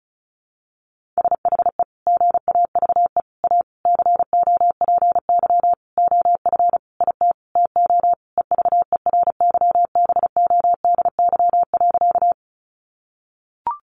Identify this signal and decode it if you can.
Morse code, heard as a keyed tone, about 34 wpm, 700 Hz. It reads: SHE GAVE A COPY OF IT TO EVERYBODY.
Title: Sentences from Top 1400 Words 35wpm